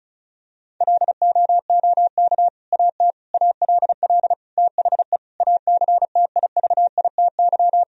Morse code, heard as a keyed tone, about 34 wpm, 700 Hz. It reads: LOOK AT ALL THE ACTIVITY
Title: Sentences from Top 1000 Words - Review Part 4 - 35wpm